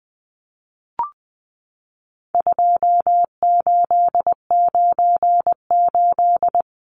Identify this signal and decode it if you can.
Morse code, heard as a keyed tone, about 20 wpm, 700 Hz. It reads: 2898
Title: Numbers - 4-Digit 20wpm